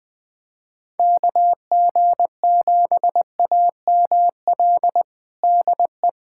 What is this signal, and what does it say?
Morse code, heard as a keyed tone, about 21 wpm, 700 Hz. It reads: KG7AML DE
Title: USA Callsign DE Callsign 20wpm